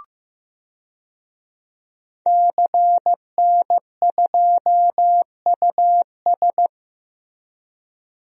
Morse code, heard as a keyed tone, about 15 wpm, 700 Hz. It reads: CN2US
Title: DXCC-100 Call Signs - 5 Characters - Repeated 2x - 15wpm